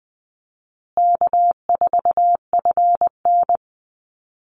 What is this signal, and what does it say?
Morse code, heard as a keyed tone, about 20 wpm, 700 Hz. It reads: K4FN
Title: Call Signs Repeated 2x - 20wpm